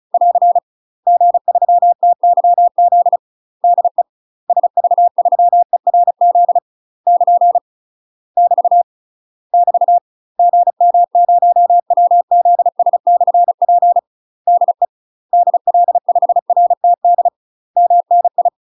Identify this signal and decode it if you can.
Morse code, heard as a keyed tone, about 36 wpm, 700 Hz. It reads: <AR> G3TYZ DE SV3ERZ <KN> <BT> <BT> GM0WZS/P DE DL5RTD MNI
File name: UK MASTER QSOs TRAINING / 35WPM / UK MASTER QSOs 0027